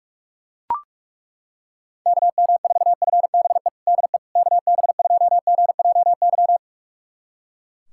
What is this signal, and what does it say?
Morse code, heard as a keyed tone, about 41 wpm, 700 Hz. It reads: KM4FBE DE KB2GJY
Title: USA Callsign DE Callsign - 6 Characters - 45wpm